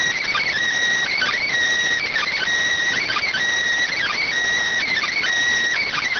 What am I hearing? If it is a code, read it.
Emergency Alert System (SAME) tones, heard as harsh buzzing bursts. It is heard from 1.1 to 5.2 s.